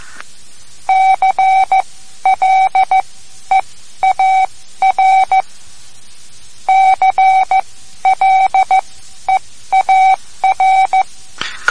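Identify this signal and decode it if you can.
Morse code, heard as a keyed tone, about 15 wpm, spaced as for 12 wpm, 760 Hz. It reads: CLEAR CLEAR
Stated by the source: Morse code lesson, third lesson in a series for learning the Morse alphabet.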